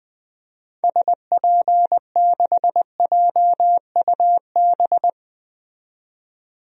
Morse code, heard as a keyed tone, about 20 wpm, 700 Hz. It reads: SP6JUB
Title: DXCC-100 Call Signs - 6 Characters - 20wpm